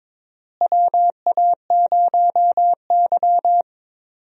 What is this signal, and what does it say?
Morse code, heard as a keyed tone, about 22 wpm, 700 Hz. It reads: WA0Y
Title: USA Call Signs - 4 Characters - Repeated 3x - 22wpm